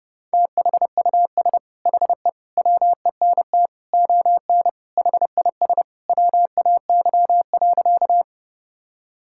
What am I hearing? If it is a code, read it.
Morse code, heard as a keyed tone, about 28 wpm, 700 Hz. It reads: THUS HE WENT ON HIS WAY.
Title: Sentences from Top 1000 Words - Review Part 5 - 30wpm